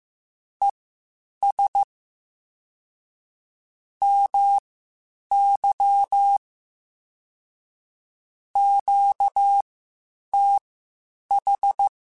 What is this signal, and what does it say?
Morse code, heard as a keyed tone, about 15 wpm, spaced as for 8 wpm, 800 Hz. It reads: ES MY QTH
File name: (spaced) / QSOs Spaced-mp3s / WD0BGP 1 @13 sp